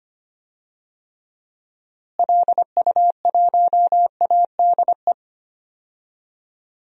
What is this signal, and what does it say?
Morse code, heard as a keyed tone, about 25 wpm, 700 Hz. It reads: LU1ADE
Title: DXCC-100 Call Signs - 6 Characters - Repeated 2x - 25wpm